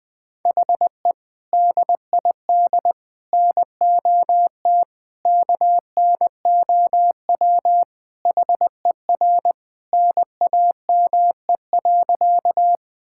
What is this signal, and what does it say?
Morse code, heard as a keyed tone, about 21 wpm, 700 Hz. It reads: HE DID NOT KNOW HER NAME.
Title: Sentences with 4-Word Phrases - Spoken Beginning - 20wpm